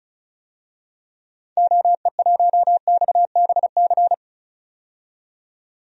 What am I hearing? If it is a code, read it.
Morse code, heard as a keyed tone, about 34 wpm, 700 Hz. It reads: OE1XBC
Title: DXCC-100 Call Signs - 6 Characters - 35wpm